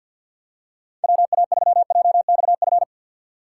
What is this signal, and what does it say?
Morse code, heard as a keyed tone, about 50 wpm, 700 Hz. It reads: WA3JXF
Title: USA Call Signs - 2x Character Spacing - 50wpm